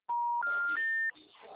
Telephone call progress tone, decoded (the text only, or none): special information tone